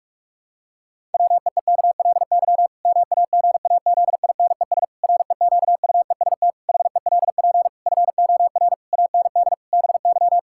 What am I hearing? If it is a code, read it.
Morse code, heard as a keyed tone, about 46 wpm, 700 Hz. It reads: WEEKLY MAGAZINES REQUEST HELP FOR AND BY
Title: Pangram Challenge v2 - 45wpm